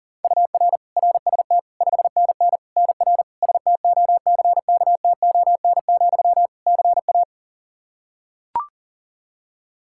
Morse code, heard as a keyed tone, about 40 wpm, 700 Hz. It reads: UR RST 5NN NR STOCKTON, CA
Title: Practice QSO - Exchange 1 - Beginning - 40wpm